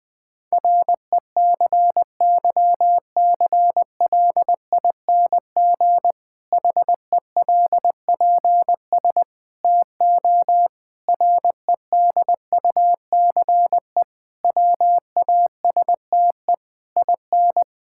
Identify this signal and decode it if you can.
Morse code, heard as a keyed tone, about 20 wpm, 700 Hz. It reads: RECYCLING HELPS TO REDUCE WASTE IN
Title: Difficult Sentences with Hints - 20wpm